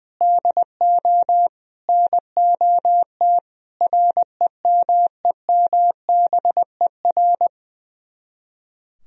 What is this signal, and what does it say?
Morse code, heard as a keyed tone, about 21 wpm, 700 Hz. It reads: DO NOT REMEMBER
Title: Speed-Racing - Sets of 4 Words 20wpm